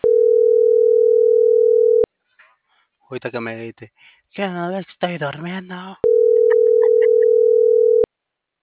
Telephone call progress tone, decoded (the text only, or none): ringback tone